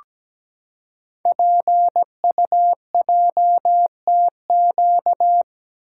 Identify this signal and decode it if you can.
Morse code, heard as a keyed tone, about 17 wpm, 700 Hz. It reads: PUJTQ
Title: Code Groups - 5 Letter:Numbers - 17wpm